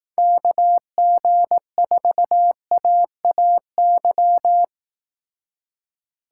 Morse code, heard as a keyed tone, about 18 wpm, 700 Hz. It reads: KG4AAY
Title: Speed Racing - USA 2x3 Call Signs 15wpm